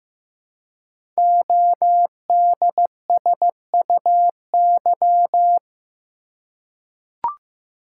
Morse code, heard as a keyed tone, about 15 wpm, 700 Hz. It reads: ODSUY
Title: Code Groups - 5 Letters - 15wpm